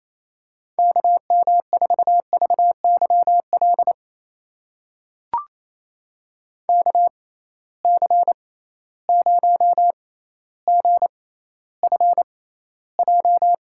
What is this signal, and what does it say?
Morse code, heard as a keyed tone, about 28 wpm, spaced as for 14 wpm, 700 Hz. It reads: KM4VYL KC0GFJ
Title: USA Call Signs - 6 Characters - 6x Character Spacing - 28wpm